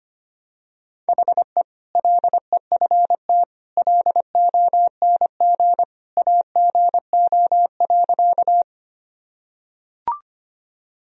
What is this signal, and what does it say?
Morse code, heard as a keyed tone, about 25 wpm, 700 Hz. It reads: HE LEFT LONG AGO.
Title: Sentences from Top 1000 Words - Review Part 3 - 25wpm